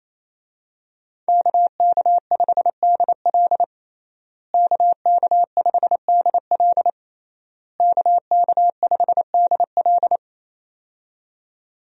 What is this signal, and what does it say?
Morse code, heard as a keyed tone, about 27 wpm, 700 Hz. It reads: KK5DL KK5DL KK5DL
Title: USA Call Signs - 5 Characters - Repeated 3x - 28wpm